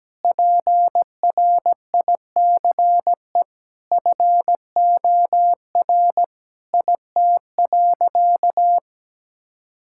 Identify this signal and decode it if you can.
Morse code, heard as a keyed tone, about 17 wpm, 700 Hz. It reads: PRICE FOR IT.
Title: Sentences from Top 1000 Words - Review Part 6 - 17wpm